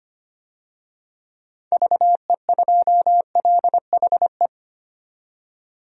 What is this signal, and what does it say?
Morse code, heard as a keyed tone, about 25 wpm, 700 Hz. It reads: VE2LHE
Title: DXCC-100 Call Signs - 6 Characters - 25wpm